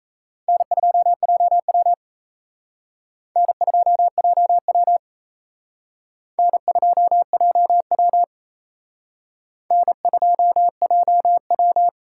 Morse code, heard as a keyed tone, about 38 wpm, 700 Hz. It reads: N2JW N2JW N2JW N2JW